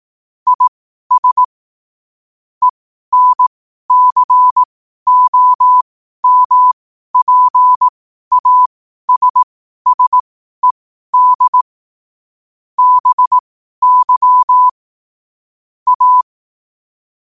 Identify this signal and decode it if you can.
Morse code, heard as a keyed tone, about 18 wpm, spaced as for 12 wpm, 1000 Hz. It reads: IS ENCOMPASSED BY A
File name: qotdfiles / 12wpm